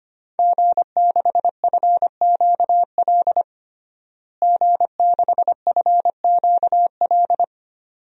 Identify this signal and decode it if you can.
Morse code, heard as a keyed tone, about 25 wpm, 700 Hz. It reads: G6FQL G6FQL
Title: DXCC-100 Call Signs - 5 Characters - Repeated 2x - 25wpm